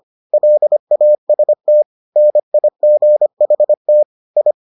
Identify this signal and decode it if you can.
Morse code, heard as a keyed tone, about 25 wpm, 600 Hz. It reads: LAST NIGHT I